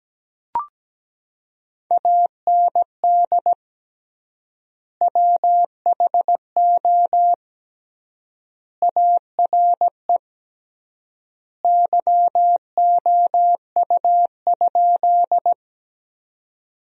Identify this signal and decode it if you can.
Morse code, heard as a keyed tone, about 17 wpm, 700 Hz. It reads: AND WHO ARE YOU?
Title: 3x Word-Spacing - Sentences from Top 100 Words - 17wpm